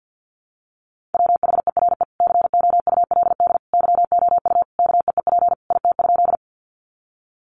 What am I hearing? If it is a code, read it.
Morse code, heard as a keyed tone, about 50 wpm, 700 Hz. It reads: WHERE COULD YOU KEEP IT?
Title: Sentences from Top 300 Words 50wpm